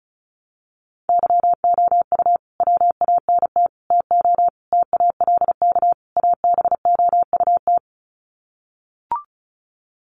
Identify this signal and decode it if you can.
Morse code, heard as a keyed tone, about 34 wpm, 700 Hz. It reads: YOU WANT TO TALK ABOUT
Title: Sets of 5 Words Encore 35wpm